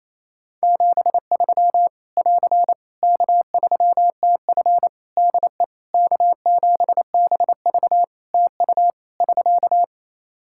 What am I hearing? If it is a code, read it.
Morse code, heard as a keyed tone, about 28 wpm, 700 Hz. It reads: 73 <AR> K3TF DE K7BV TU <SK>
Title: Practice QSO - Exchange 4 - Ending - 28wpm